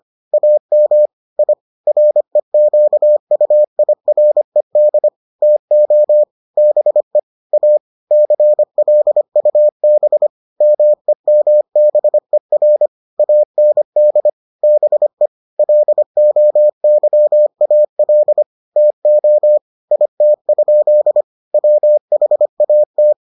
Morse code, heard as a keyed tone, about 25 wpm, 600 Hz. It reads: AM I REQUIRED TO BE A CLUB MEMBER AND BE LOYAL TO IT? WHAT